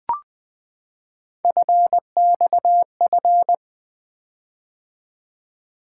Morse code, heard as a keyed tone, about 20 wpm, 700 Hz. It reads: FXF
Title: Three Letter-Number Combinations - 20wpm